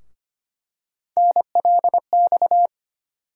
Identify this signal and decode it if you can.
Morse code, heard as a keyed tone, about 25 wpm, 700 Hz. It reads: NLX
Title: USA Call Sign Postfixes - 3 Letters 25wpm